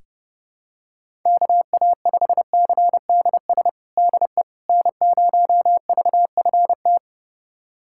Morse code, heard as a keyed tone, about 30 wpm, 700 Hz. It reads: KA5CDS DE N0VFT
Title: USA Callsign DE Callsign 30wpm